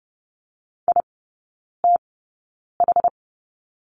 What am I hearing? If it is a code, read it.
Morse code, heard as a keyed tone, about 30 wpm, spaced as for 9 wpm, 700 Hz. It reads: ITH
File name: Lesson 16e - Letter c - Three Characters 30wpm